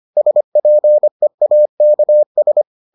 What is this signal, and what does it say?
Morse code, heard as a keyed tone, about 25 wpm, 600 Hz. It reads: SPEAKS